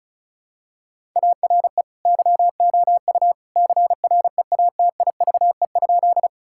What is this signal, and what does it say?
Morse code, heard as a keyed tone, about 36 wpm, 700 Hz. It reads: ARE YOU CREATIVE?
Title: Sentences from Top 1000 Words - Review Part 5 - 35wpm